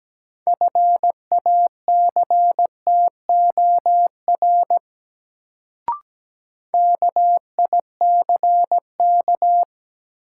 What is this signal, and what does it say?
Morse code, heard as a keyed tone, about 17 wpm, 700 Hz. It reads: FACTOR KICK
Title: Sentences from Top 1300 Words - New Word Drill - 17wpm